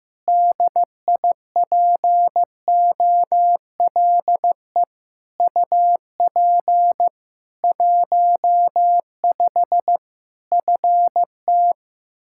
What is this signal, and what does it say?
Morse code, heard as a keyed tone, about 15 wpm, 700 Hz. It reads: DIPOLE UP 15 FT
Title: Practice QSO - Exchange 2 - Info - 15wpm